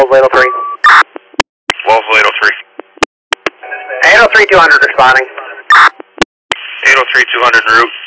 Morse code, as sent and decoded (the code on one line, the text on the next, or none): none